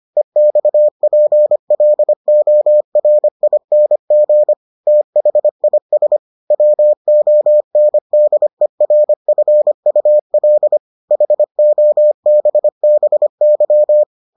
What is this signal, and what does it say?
Morse code, heard as a keyed tone, about 25 wpm, 600 Hz. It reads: EXPLORING THIS WONDERFUL HOBBY